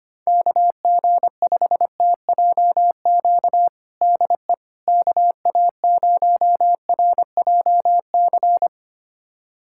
Morse code, heard as a keyed tone, about 25 wpm, 700 Hz. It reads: KG5TJQ DE KA0RJC
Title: USA Callsign DE Callsign - 6 Characters - 25wpm